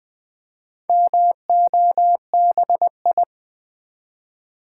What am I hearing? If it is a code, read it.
Morse code, heard as a keyed tone, about 20 wpm, 700 Hz. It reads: MOBI
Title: Build A Word - 2 Letters At A Time - 6-Letter Words - 20wpm